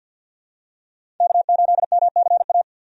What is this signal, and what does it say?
Morse code, heard as a keyed tone, about 49 wpm, 700 Hz. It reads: K8MCA
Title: USA Call Signs - 5 Characters - Repeated 2x - 50wpm